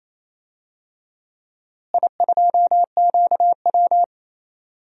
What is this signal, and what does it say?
Morse code, heard as a keyed tone, about 27 wpm, 700 Hz. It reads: I2QW
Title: DXCC-100 Call Signs - 4 Characters - Repeated 3x - 28wpm